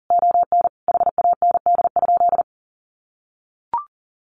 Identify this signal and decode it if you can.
Morse code, heard as a keyed tone, about 40 wpm, 700 Hz. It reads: ON HAND?
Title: Sentences from Top 1500 Words 40wpm